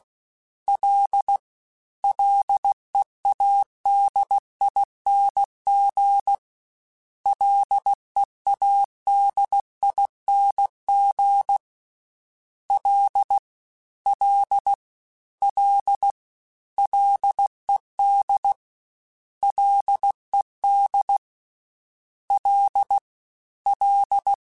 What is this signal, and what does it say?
Morse code, heard as a keyed tone, about 16 wpm, 800 Hz. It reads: L LEADING LEADING L L L LED LED L L
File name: Ls / Single L m3s / Single L @15